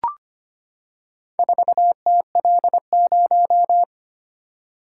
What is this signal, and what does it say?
Morse code, heard as a keyed tone, about 25 wpm, 700 Hz. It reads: 4TL0
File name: practice / 25wpm / Four Letter-Number Combinations - 25wpm